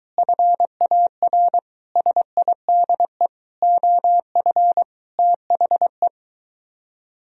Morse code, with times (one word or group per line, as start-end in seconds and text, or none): 0.2-1.6 FAR
2.0-3.3 SIDE
3.6-4.8 OF
5.2-6.1 THE